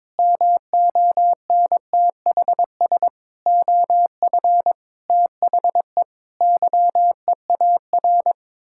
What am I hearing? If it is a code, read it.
Morse code, heard as a keyed tone, about 22 wpm, 700 Hz. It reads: MONTHS OF THE YEAR
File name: practice / 22wpm / Sets of 8 Words - 22wpm